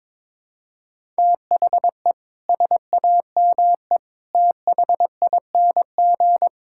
Morse code, heard as a keyed tone, about 22 wpm, 700 Hz. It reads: THE SAME THING
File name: practice / 22wpm / Sets of 6 Words - 22wpm